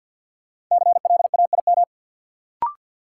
Morse code, heard as a keyed tone, about 49 wpm, 700 Hz. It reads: CLAIM
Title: Sentences from Top 1500 Words - New Word Drill Alt - 50wpm